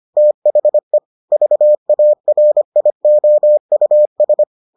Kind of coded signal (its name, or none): Morse code